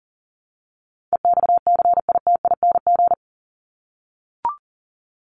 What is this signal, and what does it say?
Morse code, heard as a keyed tone, about 40 wpm, 700 Hz. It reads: EXCITING
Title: Top 1751-2000 Words 40wpm